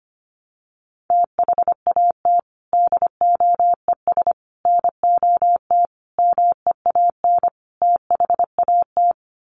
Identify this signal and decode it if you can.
Morse code, heard as a keyed tone, about 25 wpm, 700 Hz. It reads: THAT DOES NOT MEAN THAT
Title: Sets of 5 Words Encore 25wpm